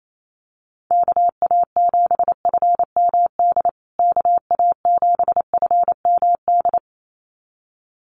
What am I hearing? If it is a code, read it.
Morse code, heard as a keyed tone, about 28 wpm, 700 Hz. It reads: KA7FMD KA7FMD
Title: USA Call Signs Repeated 2x 28wpm